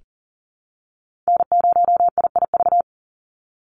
Morse code, heard as a keyed tone, about 40 wpm, 690 Hz. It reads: N0IIV
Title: USA Call Signs 40wpm